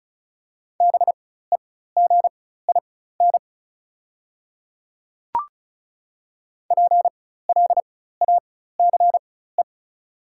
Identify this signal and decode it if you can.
Morse code, heard as a keyed tone, about 36 wpm, spaced as for 16 wpm, 700 Hz. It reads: BEGIN PLACE
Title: 4x Character Spacing - Top 500 Words - 35wpm